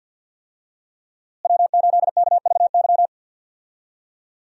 Morse code, heard as a keyed tone, about 52 wpm, 700 Hz. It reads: W8KVY
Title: USA Call Signs - 5 Characters - Repeated 2x - 50wpm